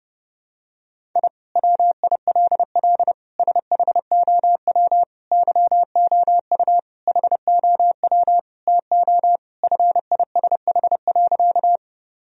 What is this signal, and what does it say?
Morse code, heard as a keyed tone, about 32 wpm, 700 Hz. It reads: I WILL SHOW YOU HOW TO FISH.
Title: Sentences from Top 1000 Words - Review Part 1 - 30wpm